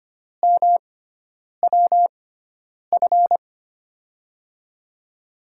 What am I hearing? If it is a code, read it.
Morse code, heard as a keyed tone, about 25 wpm, spaced as for 9 wpm, 700 Hz. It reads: MWF
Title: USA Call Signs - 6 Characters - 6x Character Spacing - 25wpm